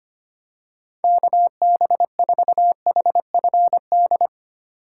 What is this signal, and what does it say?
Morse code, heard as a keyed tone, about 25 wpm, 700 Hz. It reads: KB4HFD
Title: USA Call Signs - 4x Character Spacing - 25wpm